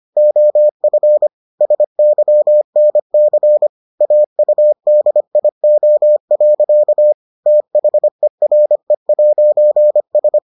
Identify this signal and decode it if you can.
Morse code, heard as a keyed tone, about 25 wpm, 600 Hz. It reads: OF SYNC AUDIO. THERE'S